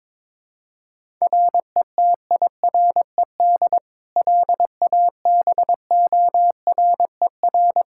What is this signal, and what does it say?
Morse code, heard as a keyed tone, about 22 wpm, 700 Hz. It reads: RETIRED LABORER
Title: Practice QSO - Exchange 3 - More Info - 22wpm